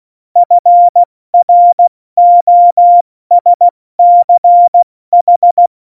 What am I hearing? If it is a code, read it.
Morse code, heard as a keyed tone, about 16 wpm, 700 Hz. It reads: FROSCH
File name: Morsekurs / Witze / Witz 11. September 2018, 70 F80